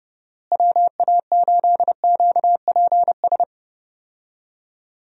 Morse code, heard as a keyed tone, about 30 wpm, 700 Hz. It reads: WA8QPS
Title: USA Call Signs - 6 Characters - Repeated 3x - 30wpm